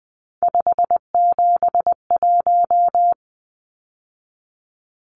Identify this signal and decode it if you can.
Morse code, heard as a keyed tone, about 20 wpm, 700 Hz. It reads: 571
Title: Numbers - 3-Digit 20wpm